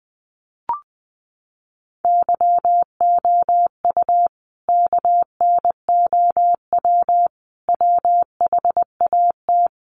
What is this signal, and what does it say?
Morse code, heard as a keyed tone, about 20 wpm, 700 Hz. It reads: YOU KNOW WHAT
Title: Sets of 5 Words 20wpm